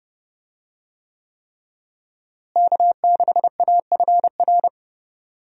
Morse code, heard as a keyed tone, about 30 wpm, 700 Hz. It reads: K6AFR
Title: USA Call Signs - 5 Characters - Repeated 3x - 30wpm